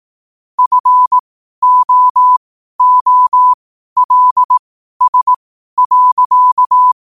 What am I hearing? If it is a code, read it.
Morse code, heard as a keyed tone, about 18 wpm, spaced as for 12 wpm, 1000 Hz. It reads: FOOLS.
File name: qotdfiles / 12wpm